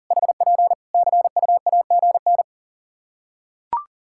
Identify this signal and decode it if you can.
Morse code, heard as a keyed tone, about 37 wpm, 700 Hz. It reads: HP CUAGN
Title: K1USN SST QSO Practice 40wpm